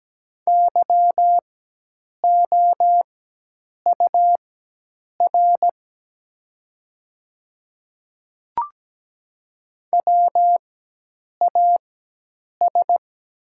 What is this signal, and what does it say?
Morse code, heard as a keyed tone, about 17 wpm, spaced as for 8 wpm, 700 Hz. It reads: YOUR WAS
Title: 4x Character Spacing - Top 100 Words - 17wpm